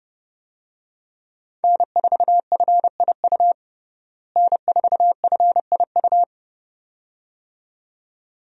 Morse code, heard as a keyed tone, about 30 wpm, 700 Hz. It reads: N4FIU N4FIU
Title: USA Call Signs - 5 Characters - Repeated 2x - 30wpm